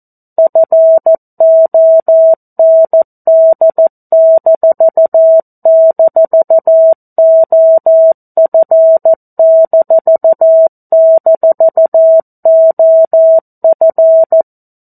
Morse code, heard as a keyed tone, about 14 wpm, 650 Hz. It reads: FOND--OF--OF
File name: 02-Chapter-02 14 WPM